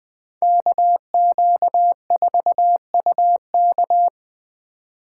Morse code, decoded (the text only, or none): KQ4UK